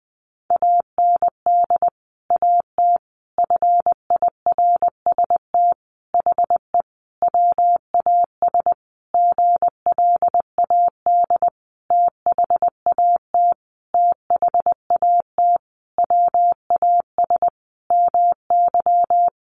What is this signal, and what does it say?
Morse code, heard as a keyed tone, about 20 wpm, 700 Hz. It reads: AND AT FIRST HE WAS GLAD THAT THAT WAS MY
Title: Sentences from Top 1300 Words 20wpm